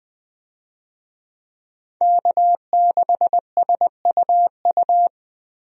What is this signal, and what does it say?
Morse code, heard as a keyed tone, about 20 wpm, 700 Hz. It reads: K6SUU
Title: USA Call Signs - 5 Characters - Repeated 3x - 20wpm